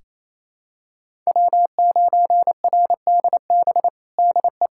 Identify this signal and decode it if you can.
Morse code, heard as a keyed tone, about 27 wpm, 700 Hz. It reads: W9RDB DE
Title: USA Callsign DE Callsign - 5 Characters - 28wpm